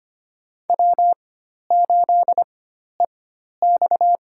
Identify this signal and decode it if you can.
Morse code, heard as a keyed tone, about 25 wpm, spaced as for 12 wpm, 700 Hz. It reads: W8EX
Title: USA Call Signs - 4 Characters - 4x Character Spacing - 25wpm